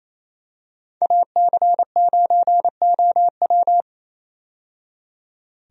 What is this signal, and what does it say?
Morse code, heard as a keyed tone, about 28 wpm, 700 Hz. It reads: AC9OW
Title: USA Call Signs - 5 Characters - 28wpm